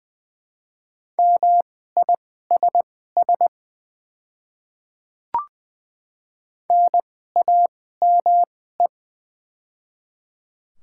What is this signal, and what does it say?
Morse code, heard as a keyed tone, about 20 wpm, spaced as for 15 wpm, 700 Hz. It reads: MISS NAME
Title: 2x Character Spacing - Top 500 Words - 20wpm